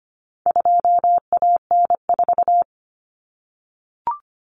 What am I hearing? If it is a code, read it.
Morse code, heard as a keyed tone, about 25 wpm, 700 Hz. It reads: 2AN4
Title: Common Cut Numbers - 4-Digit 25wpm